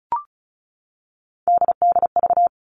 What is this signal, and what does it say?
Morse code, heard as a keyed tone, about 34 wpm, 700 Hz. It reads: DDV
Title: All Cut Numbers - 3-Digit 35wpm